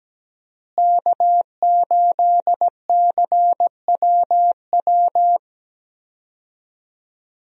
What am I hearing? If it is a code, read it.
Morse code, heard as a keyed tone, about 17 wpm, 700 Hz. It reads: K8CWW
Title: USA Call Signs - 5 Characters - Repeated 2x - 17wpm